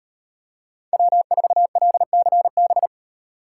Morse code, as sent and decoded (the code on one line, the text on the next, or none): .-- ....- .-.. -.-. -...
W4LCB